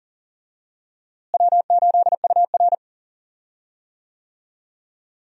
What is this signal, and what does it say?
Morse code, heard as a keyed tone, about 39 wpm, 700 Hz. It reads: W8UR
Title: USA Call Signs - 4 Characters - Repeated 2x - 40wpm